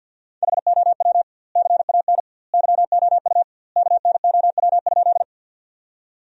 Morse code, heard as a keyed tone, about 50 wpm, 700 Hz. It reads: HOW CAN YOU KNOW?
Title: 2x Word-Spacing - Sentences from Top 100 Words - 50wpm